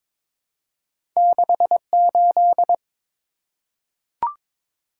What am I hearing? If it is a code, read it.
Morse code, heard as a keyed tone, about 22 wpm, 700 Hz. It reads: 68
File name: practice / 22wpm / Two Letter-Number Combinations - 22wpm